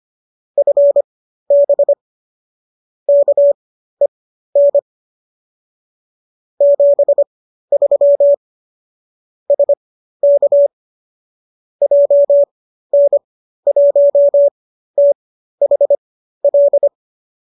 Morse code, heard as a keyed tone, about 25 wpm, spaced as for 13 wpm, 570 Hz. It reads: FB KEN 73 SK JN1THL